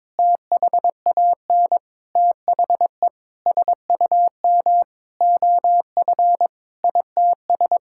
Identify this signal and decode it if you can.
Morse code, heard as a keyed tone, about 21 wpm, 700 Hz. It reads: THAN THE SUM OF ITS
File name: Sentences with 7-Word Phrases - 22wpm